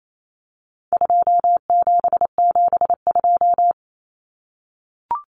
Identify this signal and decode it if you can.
Morse code, heard as a keyed tone, about 28 wpm, 700 Hz. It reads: 2772
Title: Common Cut Numbers - 4-Digit 28wpm